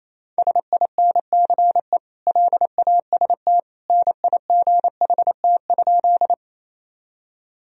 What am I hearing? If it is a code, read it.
Morse code, heard as a keyed tone, about 28 wpm, 700 Hz. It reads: SINCE LAST NIGHT?
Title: Sentences from Top 1000 Words - Review Part 1 - 28wpm